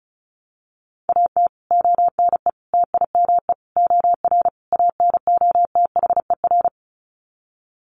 Morse code, heard as a keyed tone, about 34 wpm, 700 Hz. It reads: AT ONE TIME OR ANOTHER
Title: Sets of 5 Words Encore 35wpm